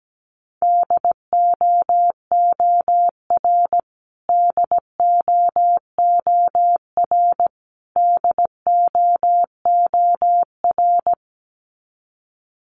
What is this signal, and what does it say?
Morse code, heard as a keyed tone, about 17 wpm, 700 Hz. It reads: DOOR DOOR DOOR
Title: Top 300 Words Repeated 17wpm